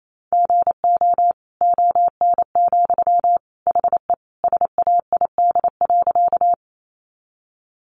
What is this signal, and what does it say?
Morse code, heard as a keyed tone, about 28 wpm, 700 Hz. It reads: GO ON, HE SAID.
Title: Sentences from Top 100 Words 28wpm